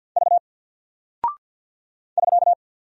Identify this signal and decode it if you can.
Morse code, heard as a keyed tone, about 48 wpm, 700 Hz. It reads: V <SK>
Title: Single Character (Letter:Number:Punctuation:Prosign) Alt 50wpm